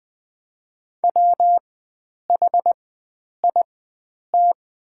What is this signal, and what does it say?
Morse code, heard as a keyed tone, about 20 wpm, spaced as for 9 wpm, 700 Hz. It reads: WHIT
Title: 4x Character Spacing - Top 300 Words - 20wpm